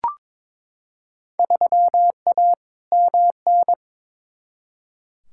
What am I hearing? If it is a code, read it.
Morse code, heard as a keyed tone, about 22 wpm, 700 Hz. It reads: 3A MN
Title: Field Day Exchanges - 22wpm